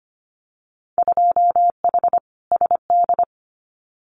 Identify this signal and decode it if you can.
Morse code, heard as a keyed tone, about 25 wpm, 700 Hz. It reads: 2H SD
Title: Winter Field Day Exchanges 25wpm